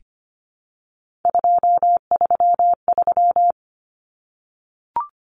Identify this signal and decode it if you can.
Morse code, heard as a keyed tone, about 25 wpm, 700 Hz. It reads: 233
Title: Common Cut Numbers - 3-Digit 25wpm